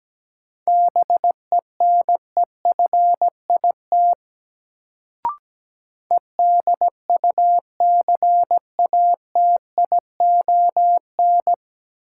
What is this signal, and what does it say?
Morse code, heard as a keyed tone, about 17 wpm, 700 Hz. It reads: BENEFIT EDUCATION
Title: Sentences from Top 1000 Words - Review Part 6 - New Word Drill - 17wpm